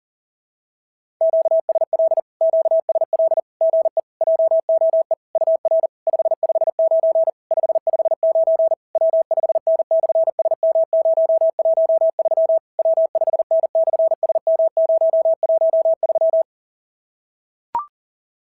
Morse code, heard as a keyed tone, about 40 wpm, 690 Hz. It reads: QSL QSL GE JOE UR 559 559 W5N/SM013 W5N/SM013
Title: SOTA QSO Practice 40wpm